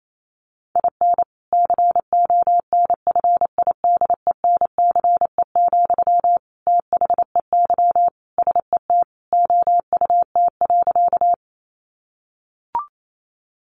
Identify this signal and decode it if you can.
Morse code, heard as a keyed tone, about 28 wpm, 700 Hz. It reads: IN CONFIDENCE, THEY SET OUT.
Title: Sentences from Top 1500 Words 28wpm